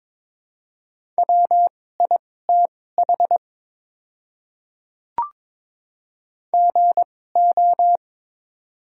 Morse code, heard as a keyed tone, about 22 wpm, spaced as for 16 wpm, 700 Hz. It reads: WITH GO
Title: 2x Character Spacing - Top 100 Words - 22wpm